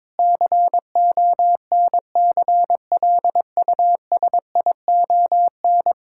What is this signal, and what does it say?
Morse code, heard as a keyed tone, about 22 wpm, 700 Hz. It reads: CONCLUSION
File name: practice / 22wpm / Sets of 7 Words - 22wpm